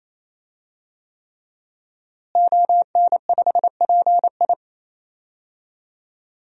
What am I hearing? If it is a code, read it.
Morse code, heard as a keyed tone, about 28 wpm, 700 Hz. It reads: ON5PI
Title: DXCC-100 Call Signs - 5 Characters - Repeated 3x - 28wpm